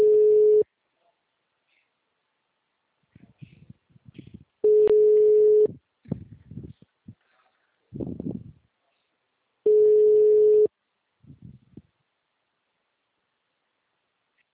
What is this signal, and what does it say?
Ringback tone (Europe, 425 Hz).
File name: out- -2002- - - .239